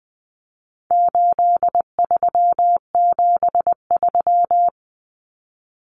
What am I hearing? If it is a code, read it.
Morse code, heard as a keyed tone, about 20 wpm, 700 Hz. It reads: 8373
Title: Numbers - 4-Digit 20wpm